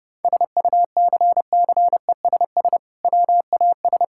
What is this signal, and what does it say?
Morse code, heard as a keyed tone, about 30 wpm, 700 Hz. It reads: SUCCESS WAS
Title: Sentences from Top 1000 Words - Review Part 6 - 30wpm